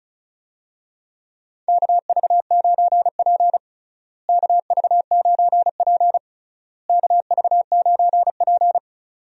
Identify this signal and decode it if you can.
Morse code, heard as a keyed tone, about 36 wpm, 700 Hz. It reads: KV9P KV9P KV9P
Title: USA Call Signs - 4 Characters - Repeated 3x - 35wpm